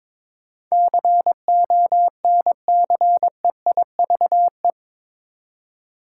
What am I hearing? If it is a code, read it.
Morse code, heard as a keyed tone, about 22 wpm, 700 Hz. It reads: CONCEIVE
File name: practice / 22wpm / Eight Letter Words - 22wpm